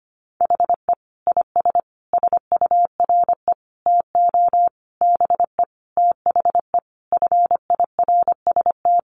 Morse code, heard as a keyed tone, about 25 wpm, 700 Hz. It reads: HE IS SURE TO BE THE FIRST
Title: Sentences from Top 1400 Words 25wpm